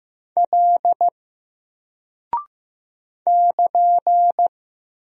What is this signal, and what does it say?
Morse code, heard as a keyed tone, about 15 wpm, 700 Hz. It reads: L <KN>
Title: Single Character (Letter:Number:Punctuation:Prosign) Alt 15wpm